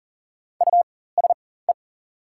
Morse code, heard as a keyed tone, about 40 wpm, spaced as for 19 wpm, 690 Hz. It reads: USE
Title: 4x Character Spacing - Top 500 Words - 40wpm